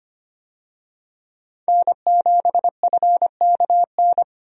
Morse code, heard as a keyed tone, about 25 wpm, 700 Hz. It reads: N7FKN
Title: USA Call Signs - 5 Characters - 25wpm